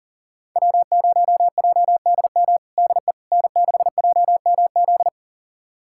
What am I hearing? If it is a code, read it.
Morse code, heard as a keyed tone, about 40 wpm, 700 Hz. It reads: W0JDM DE N6JMZ
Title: USA Callsign DE Callsign 40wpm